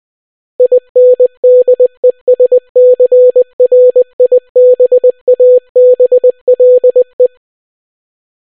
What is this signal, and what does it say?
Morse code, heard as a keyed tone, about 20 wpm, 500 Hz. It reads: INDESCRIBABLE